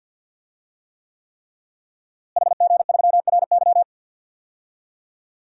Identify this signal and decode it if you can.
Morse code, heard as a keyed tone, about 45 wpm, 700 Hz. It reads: HG3RY
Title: DXCC-100 Call Signs - 5 Characters - Repeated 3x - 50wpm